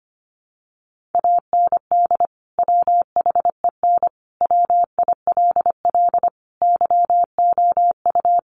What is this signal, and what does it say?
Morse code, heard as a keyed tone, about 25 wpm, 700 Hz. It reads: AND WHEN WILL YOU
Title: Sentences from Top 100 Words 25wpm